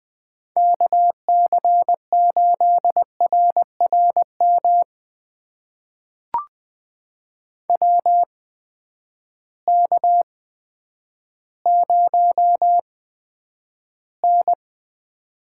Morse code, heard as a keyed tone, about 20 wpm, spaced as for 10 wpm, 700 Hz. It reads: KC8RRM WK0N
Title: USA Call Signs - 8x Character Spacing - 20wpm